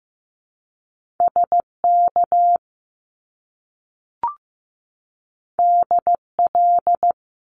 Morse code, heard as a keyed tone, about 15 wpm, 700 Hz. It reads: SK DL
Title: Top Two Letter Combinations in Words 15wpm